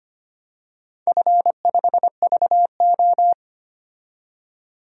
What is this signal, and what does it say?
Morse code, heard as a keyed tone, about 25 wpm, 700 Hz. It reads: F5VO
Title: DXCC-100 Call Signs - 4 Characters - 25wpm